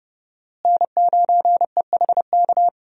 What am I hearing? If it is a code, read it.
Morse code, heard as a keyed tone, about 31 wpm, 700 Hz. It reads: N9EHK
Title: USA Call Signs - 5 Characters - Repeated 3x - 30wpm